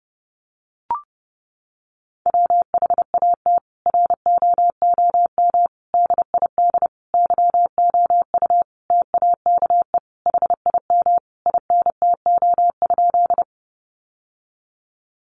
Morse code, heard as a keyed tone, about 30 wpm, 700 Hz. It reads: WHAT ROOM DID YOU TAKE HIM INTO?
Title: Sentences from Top 600 Words 30wpm